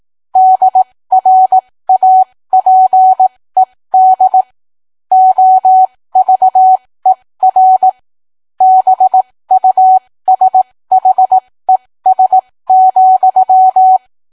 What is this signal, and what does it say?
Morse code, heard as a keyed tone, about 18 wpm, spaced as for 15 wpm, 750 Hz. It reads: DRAPED OVER BUSHES,